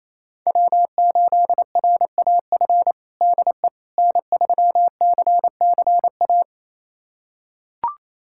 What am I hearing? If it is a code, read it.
Morse code, heard as a keyed tone, about 28 wpm, 700 Hz. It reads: W8RAF DE N3CCA
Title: USA Callsign DE Callsign - 5 Characters - 28wpm